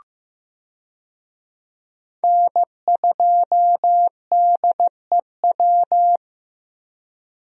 Morse code, heard as a keyed tone, about 15 wpm, 700 Hz. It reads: N2DEW
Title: USA Call Signs - 5 Characters - Repeated 3x - 15wpm